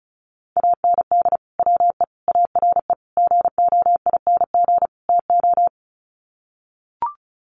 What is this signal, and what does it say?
Morse code, heard as a keyed tone, about 34 wpm, 700 Hz. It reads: AND WE ARE GOING TO
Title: Sets of 5 Words Encore2 35wpm